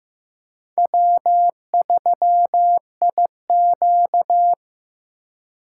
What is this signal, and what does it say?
Morse code, heard as a keyed tone, about 15 wpm, 700 Hz. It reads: W3IQ
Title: USA Call Signs - 4 Characters - 3x Character Spacing - 15wpm